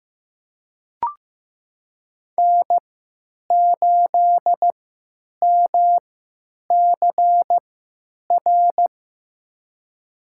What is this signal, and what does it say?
Morse code, heard as a keyed tone, about 15 wpm, spaced as for 9 wpm, 700 Hz. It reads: N8MCR
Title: USA Call Signs - 5 Characters - 3x Character Spacing - 15wpm